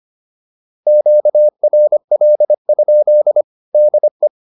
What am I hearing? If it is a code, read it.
Morse code, heard as a keyed tone, about 25 wpm, 600 Hz. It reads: QRL? DE